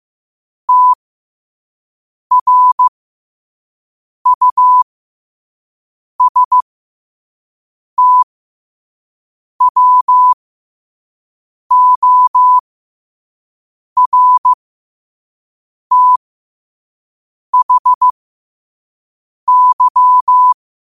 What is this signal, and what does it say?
Morse code, heard as a keyed tone, about 15 wpm, spaced as for 5 wpm, 1000 Hz. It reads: TRUSTWORTHY
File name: qotdfiles / 5wpm